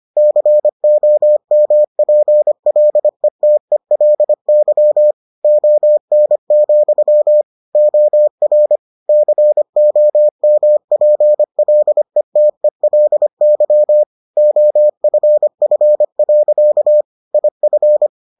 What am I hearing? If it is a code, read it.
Morse code, heard as a keyed tone, about 25 wpm, 600 Hz. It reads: COMPLETELY ON, OR COMPLETELY OFF. IF